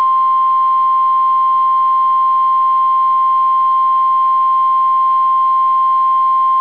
Emergency Alert System (SAME) tones (the 1050 Hz attention tone). It is heard from 0.5 to 5.7 s.